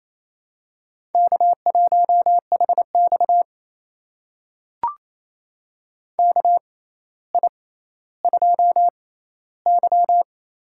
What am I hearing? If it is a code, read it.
Morse code, heard as a keyed tone, about 28 wpm, spaced as for 14 wpm, 700 Hz. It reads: K1HX KI2Y
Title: USA Call Signs - 4 Characters - 6x Character Spacing - 28wpm